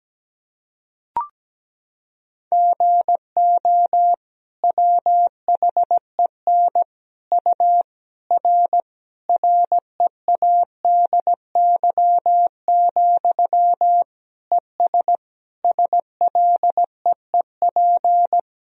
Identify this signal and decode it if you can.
Morse code, heard as a keyed tone, about 17 wpm, 700 Hz. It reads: GO WHEN U R READY, ES SLEEP
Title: Sentences from Top 500 Words and Abbreviations 17wpm